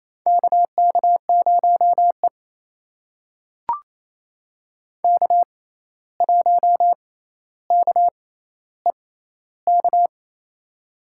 Morse code, heard as a keyed tone, about 28 wpm, spaced as for 13 wpm, 700 Hz. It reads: KK0E K1KEK
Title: USA Call Signs - 6x Character Spacing - 28wpm